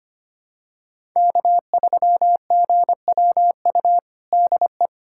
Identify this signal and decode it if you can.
Morse code, heard as a keyed tone, about 25 wpm, 700 Hz. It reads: K3GWU DE
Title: USA Callsign DE Callsign - 5 Characters - 25wpm